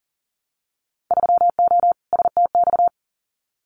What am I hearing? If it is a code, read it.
Morse code, heard as a keyed tone, about 40 wpm, 700 Hz. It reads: 3O STX
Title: Winter Field Day Exchanges 40wpm